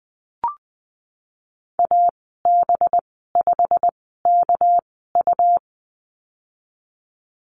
Morse code, heard as a keyed tone, about 19 wpm, spaced as for 14 wpm, 700 Hz. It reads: AB5KU
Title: 219 Lesson 35g - Number 0 - Call Signs 20wpm